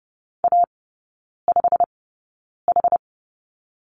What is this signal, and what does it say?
Morse code, heard as a keyed tone, about 30 wpm, spaced as for 9 wpm, 700 Hz. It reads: A5H
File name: course / 30:10wpm / 087 Lesson 15e - Number 5 - Three Characters 30wpm